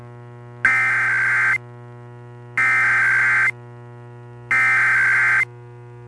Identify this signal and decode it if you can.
Emergency Alert System (SAME) tones (header bursts), heard as harsh buzzing bursts. It is heard from 0.7 to 5.4 s.